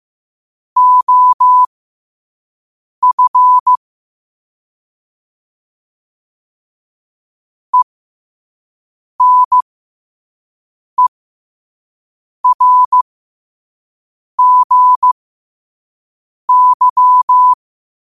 Morse code, heard as a keyed tone, about 15 wpm, spaced as for 5 wpm, 1000 Hz. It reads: OF ENERGY
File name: qotdfiles / 5wpm